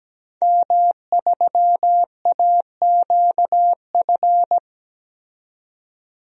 Morse code, heard as a keyed tone, about 17 wpm, 700 Hz. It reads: M3AQF
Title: DXCC-100 Call Signs - 5 Characters - 17wpm